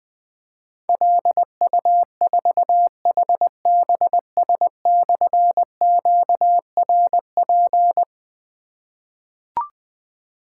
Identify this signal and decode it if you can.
Morse code, heard as a keyed tone, about 20 wpm, 700 Hz. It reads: LU4HBS/QRP
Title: Difficult Call Signs 20wpm